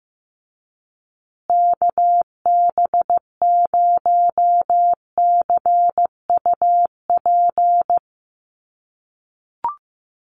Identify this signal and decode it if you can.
Morse code, heard as a keyed tone, about 15 wpm, 700 Hz. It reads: KB0CUP
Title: USA Call Signs Repeated 3x 15wpm